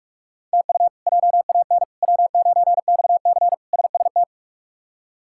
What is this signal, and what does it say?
Morse code, heard as a keyed tone, about 46 wpm, 700 Hz. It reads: TU JAN W9XC SST
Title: K1USN SST QSO Practice - 45wpm